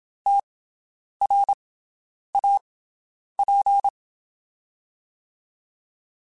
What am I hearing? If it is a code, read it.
Morse code, heard as a keyed tone, about 27 wpm, spaced as for 9 wpm, 800 Hz. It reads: TRAP
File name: QSOs Spaced